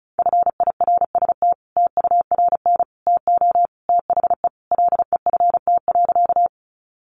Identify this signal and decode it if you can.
Morse code, heard as a keyed tone, about 34 wpm, 700 Hz. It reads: FIRST TURN TO THE LEFT.
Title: Sentences from Top 300 Words 35wpm